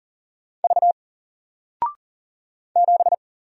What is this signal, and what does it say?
Morse code, heard as a keyed tone, about 41 wpm, 700 Hz. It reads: V 7
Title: Single Character (Letter:Number:Punctuation:Prosign) 40wpm